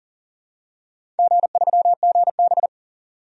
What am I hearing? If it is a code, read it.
Morse code, heard as a keyed tone, about 40 wpm, 700 Hz. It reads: G3GB
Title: DXCC-100 Call Signs - 4 Characters - 40wpm